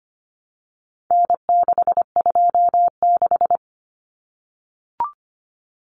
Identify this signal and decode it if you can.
Morse code, heard as a keyed tone, about 25 wpm, 700 Hz. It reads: N626
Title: Common Cut Numbers - 4-Digit 25wpm